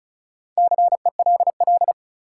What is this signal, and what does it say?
Morse code, heard as a keyed tone, about 36 wpm, 700 Hz. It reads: CELL
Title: Sentences from Top 1100 Words - New Word Drill Alt - 35wpm